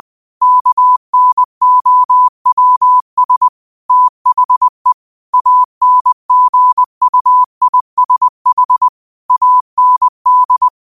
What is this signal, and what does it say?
Morse code, heard as a keyed tone, about 20 wpm, 1000 Hz. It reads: KNOWS THE ANGUISH AND